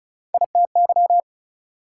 Morse code, heard as a keyed tone, about 36 wpm, 700 Hz. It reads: ITY
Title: Prefixes and Suffixes 35wpm